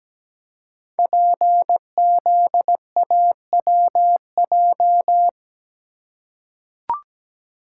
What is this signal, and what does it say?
Morse code, heard as a keyed tone, about 17 wpm, 700 Hz. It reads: PZAWJ
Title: Code Groups - 5 Letter:Numbers - 17wpm